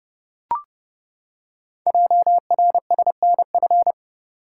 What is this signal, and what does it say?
Morse code, heard as a keyed tone, about 30 wpm, 700 Hz. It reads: JRSNF
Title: Code Groups - 5 Letters - 30wpm